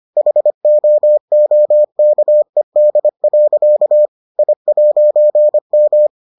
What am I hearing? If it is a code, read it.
Morse code, heard as a keyed tone, about 25 wpm, 600 Hz. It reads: HOOKED. I'M